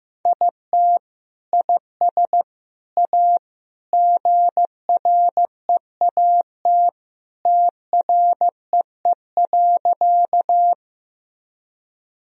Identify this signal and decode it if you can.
Morse code, heard as a keyed tone, about 16 wpm, 700 Hz. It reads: IT IS A GREAT TREE.
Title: Sentences from Top 1000 Words - Review Part 1 - 15wpm